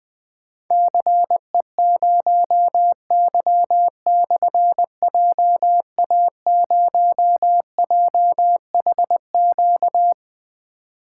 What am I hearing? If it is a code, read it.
Morse code, heard as a keyed tone, about 20 wpm, 700 Hz. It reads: CE0Y/JA0JHQ
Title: Difficult Call Signs 20wpm